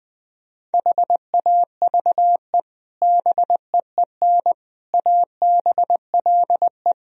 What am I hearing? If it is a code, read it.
Morse code, heard as a keyed tone, about 21 wpm, 700 Hz. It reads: HAVE BEEN ABLE
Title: Speed-Racing - Sets of 4 Words 20wpm